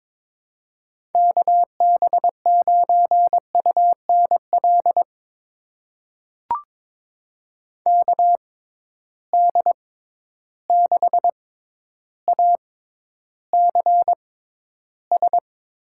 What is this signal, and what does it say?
Morse code, heard as a keyed tone, about 23 wpm, spaced as for 11 wpm, 700 Hz. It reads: KB9UNL KD6ACS
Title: USA Call Signs - 6 Characters - 6x Character Spacing - 22wpm